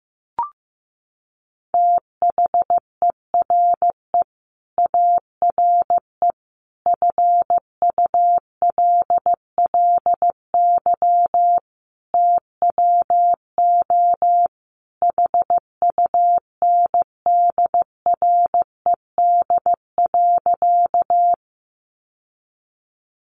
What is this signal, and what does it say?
Morse code, heard as a keyed tone, about 15 wpm, 700 Hz. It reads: THERE ARE FULLY TWO HUNDRED.
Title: Sentences from Top 1300 Words 15wpm